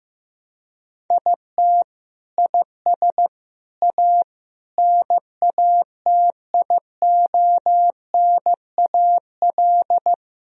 Morse code, heard as a keyed tone, about 15 wpm, 700 Hz. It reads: IT IS A NATIONAL
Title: Sentences from Top 1000 Words - Review Part 4 - 15wpm